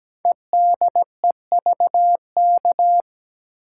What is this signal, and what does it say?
Morse code, heard as a keyed tone, about 17 wpm, 700 Hz. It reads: EDEVK
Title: Code Groups - 5 Letters - 17wpm